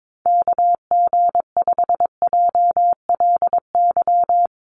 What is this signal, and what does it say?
Morse code, heard as a keyed tone, about 22 wpm, 700 Hz. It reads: KG5JLY
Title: USA Call Signs 22wpm